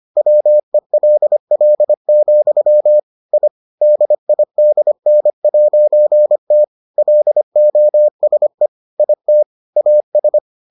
Morse code, heard as a keyed tone, about 25 wpm, 600 Hz. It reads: WELL, I DIDN'T LOSE IT AS